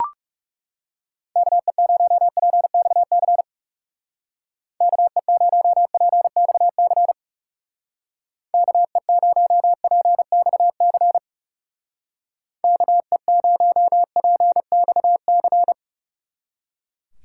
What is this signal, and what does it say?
Morse code, heard as a keyed tone, about 37 wpm, 700 Hz. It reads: KE0PXC KE0PXC KE0PXC KE0PXC
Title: Speed Racing - USA 2x3 Call Signs 30wpm